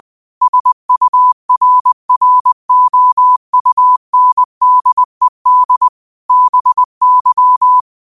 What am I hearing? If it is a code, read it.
Morse code, heard as a keyed tone, about 20 wpm, 1000 Hz. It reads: SURROUNDED BY